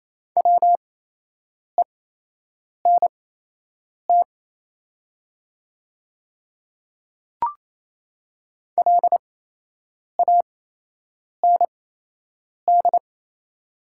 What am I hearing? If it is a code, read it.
Morse code, heard as a keyed tone, about 27 wpm, spaced as for 8 wpm, 690 Hz. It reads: WENT LAND
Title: 8x Character Spacing - Top 200 Words - 28wpm